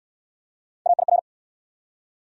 Morse code, heard as a keyed tone, about 69 wpm, 700 Hz. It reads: AIR